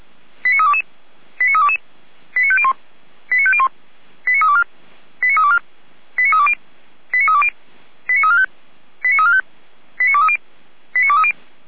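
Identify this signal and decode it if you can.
Five-tone selective call (ZVEI), heard as quick runs of five tones, tones of about 70 ms. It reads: 89322 89322 89501 89501 89324 89324 89320 89320 89345 89345 89230 89230
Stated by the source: ZVEI five-tone call received by SDR, from the Feuerwehr Orlamünde audio archive.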